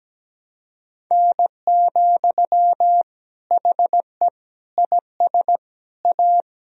Morse code, heard as a keyed tone, about 17 wpm, 700 Hz. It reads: N, HE IS A
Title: Sentences from Top 400 Words and Abbreviations 17wpm